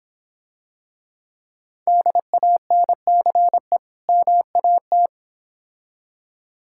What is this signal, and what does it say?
Morse code, heard as a keyed tone, about 26 wpm, 700 Hz. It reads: DANCE MAT